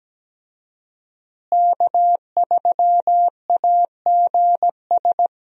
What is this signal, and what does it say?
Morse code, heard as a keyed tone, about 17 wpm, 700 Hz. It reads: K3AGS
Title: USA Call Signs - 5 Characters - Repeated 3x - 17wpm